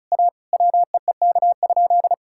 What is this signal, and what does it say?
Morse code, heard as a keyed tone, about 34 wpm, 700 Hz. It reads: A WEEK?
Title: Sentences from Top 1000 Words - Review Part 6 - 35wpm